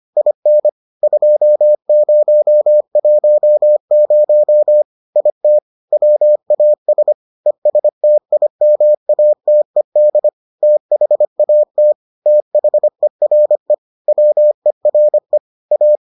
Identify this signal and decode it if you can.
Morse code, heard as a keyed tone, about 25 wpm, 600 Hz. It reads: IN 2010 IT WAS ESTIMATED THAT THERE WERE A